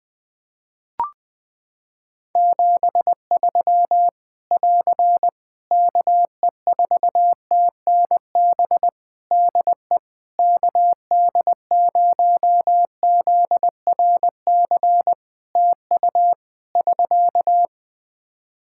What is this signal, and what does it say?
Morse code, heard as a keyed tone, about 20 wpm, 700 Hz. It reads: 73 <AR> KE4TNB DE KD0ZRC TU <SK>
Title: K1USN SST QSO Practice 20wpm